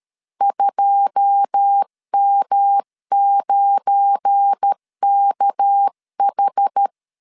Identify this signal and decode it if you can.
Morse code, heard as a keyed tone, about 13 wpm, 800 Hz. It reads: 2M9KH
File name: LEZIONI / 9 28 12wpm 2351